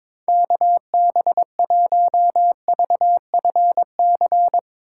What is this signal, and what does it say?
Morse code, heard as a keyed tone, about 24 wpm, 700 Hz. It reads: KB1VFC
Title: Speed Racing - USA 2x3 Call Signs 22wpm